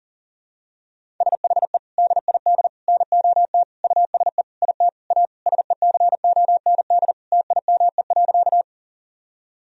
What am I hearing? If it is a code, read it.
Morse code, heard as a keyed tone, about 37 wpm, 700 Hz. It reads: SHE DID NOT USE IT A SECOND TIME.
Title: Sentences from Top 1000 Words - Review Part 1 - 40wpm